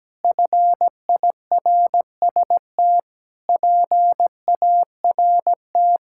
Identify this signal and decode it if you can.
Morse code, heard as a keyed tone, about 17 wpm, 700 Hz. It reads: FIRST PART
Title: Sentences from Top 1000 Words - Review Part 6 - 17wpm